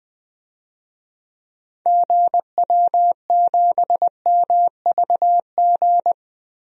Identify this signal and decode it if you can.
Morse code, heard as a keyed tone, about 20 wpm, 700 Hz. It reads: GW7MVG
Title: DXCC-100 Call Signs - 6 Characters - 20wpm